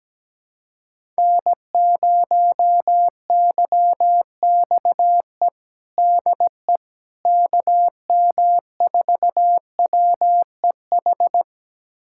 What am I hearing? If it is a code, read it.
Morse code, heard as a keyed tone, about 17 wpm, 700 Hz. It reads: N0YXE DE KM4WEH
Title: USA Callsign DE Callsign 17wpm